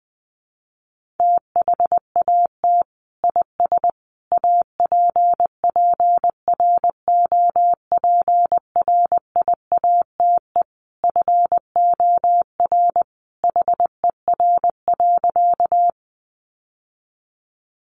Morse code, heard as a keyed tone, about 20 wpm, 700 Hz. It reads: THAT IS APPROPRIATE FOR HER.
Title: Sentences from Top 900 Words 20wpm